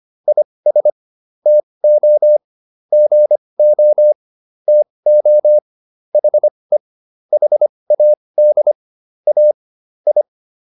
Morse code, heard as a keyed tone, about 25 wpm, spaced as for 20 wpm, 600 Hz. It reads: IS TO GO TO HE HAD A I